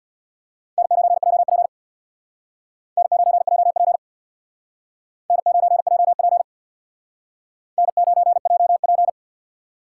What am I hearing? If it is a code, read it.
Morse code, heard as a keyed tone, about 63 wpm, 700 Hz. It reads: N9JP N9JP N9JP N9JP